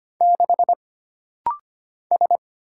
Morse code, heard as a keyed tone, about 25 wpm, 700 Hz. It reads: 6 S
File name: Single Character (Letter:Number:Punctuation:Prosign) - Rapid-Fire - 25wpm